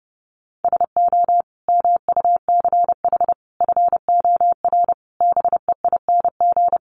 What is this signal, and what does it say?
Morse code, heard as a keyed tone, about 30 wpm, 700 Hz. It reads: SO MUCH FOR BEING
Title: Sets of 5 Words Encore 30wpm